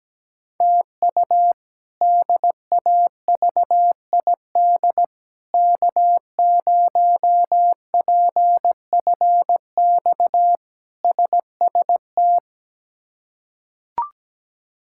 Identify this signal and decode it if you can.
Morse code, heard as a keyed tone, about 17 wpm, 700 Hz. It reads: TU DAVID K0PFX SST
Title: K1USN SST QSO Practice - 17wpm